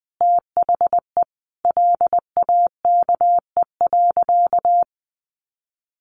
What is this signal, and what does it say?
Morse code, heard as a keyed tone, about 20 wpm, 700 Hz. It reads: THE LAKE.
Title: Sentences from Top 1100 Words 20wpm